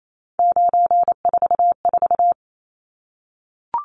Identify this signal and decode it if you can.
Morse code, heard as a keyed tone, about 28 wpm, 700 Hz. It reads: 944
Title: Numbers - 3-Digit 28wpm